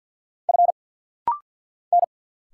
Morse code, heard as a keyed tone, about 50 wpm, 690 Hz. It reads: F N
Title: Single Letter-Number Alt - Rapid-Fire 50wpm